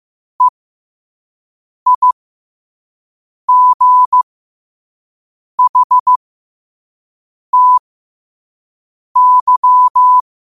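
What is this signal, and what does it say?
Morse code, heard as a keyed tone, about 15 wpm, spaced as for 5 wpm, 1000 Hz. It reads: EIGHTY
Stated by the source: Morse code practice at 5 wpm.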